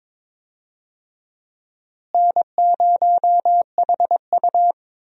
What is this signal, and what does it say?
Morse code, heard as a keyed tone, about 22 wpm, 700 Hz. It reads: N0HU
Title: USA Call Signs - 4 Characters - Repeated 3x - 22wpm